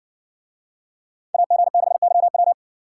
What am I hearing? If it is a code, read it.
Morse code, heard as a keyed tone, about 56 wpm, 700 Hz. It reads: AG6YW